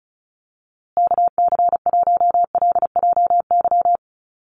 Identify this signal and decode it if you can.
Morse code, heard as a keyed tone, about 34 wpm, 700 Hz. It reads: KC1LJY
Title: USA Call Signs Repeated 3x 35wpm